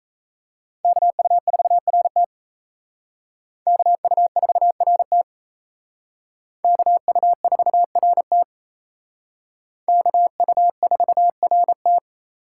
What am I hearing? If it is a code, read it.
Morse code, heard as a keyed tone, about 32 wpm, 700 Hz. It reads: KU4RT KU4RT KU4RT KU4RT
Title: Speed Racing - USA 2x2 Call Signs 28wpm